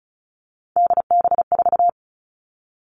Morse code, heard as a keyed tone, about 36 wpm, 700 Hz. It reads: DB4
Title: Three Challenging Letter-Number Combinations 35wpm